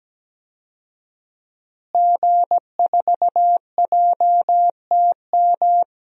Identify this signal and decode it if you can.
Morse code, heard as a keyed tone, about 17 wpm, 700 Hz. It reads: G4JTM
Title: DXCC-100 Call Signs - 5 Characters - Repeated 3x - 17wpm